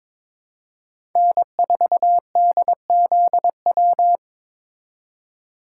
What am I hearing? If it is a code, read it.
Morse code, heard as a keyed tone, about 22 wpm, 700 Hz. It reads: N4DZW
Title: USA Call Signs - 5 Characters - 3x Character Spacing - 22wpm